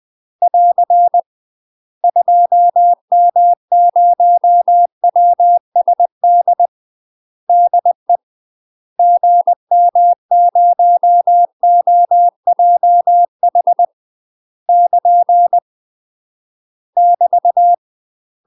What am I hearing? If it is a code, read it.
Morse code, heard as a keyed tone, about 20 wpm, 700 Hz. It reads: <AR> 2M0WSD DE GM0OJH <KN> <BT>